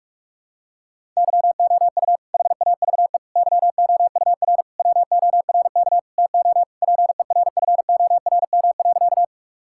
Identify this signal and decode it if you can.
Morse code, heard as a keyed tone, about 46 wpm, 700 Hz. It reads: YOU HAVE YOUR WORK TO PERFORM.
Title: Sentences from Top 1000 Words - Review Part 5 - 45wpm